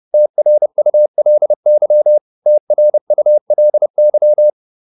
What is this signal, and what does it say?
Morse code, heard as a keyed tone, about 30 wpm, 600 Hz. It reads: TRULY TRULY